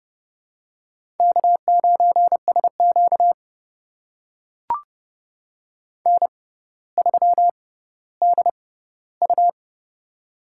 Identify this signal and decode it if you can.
Morse code, heard as a keyed tone, about 30 wpm, spaced as for 15 wpm, 700 Hz. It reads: K9SQ N3DU
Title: USA Call Signs - 4 Characters - 6x Character Spacing - 30wpm